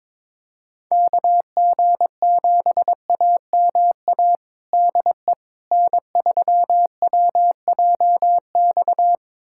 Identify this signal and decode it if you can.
Morse code, heard as a keyed tone, about 22 wpm, 700 Hz. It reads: KG7AMA DE N3WJX
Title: USA Callsign DE Callsign 22wpm